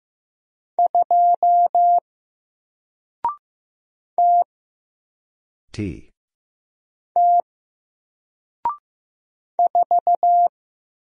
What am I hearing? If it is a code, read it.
Morse code, heard as a keyed tone, about 15 wpm, 700 Hz. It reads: 2 T T 4
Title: Single Letter-Number Alt 15wpm